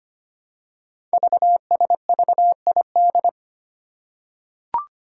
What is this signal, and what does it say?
Morse code, heard as a keyed tone, about 25 wpm, 700 Hz. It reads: VSVID
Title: Code Groups - 5 Letters - 25wpm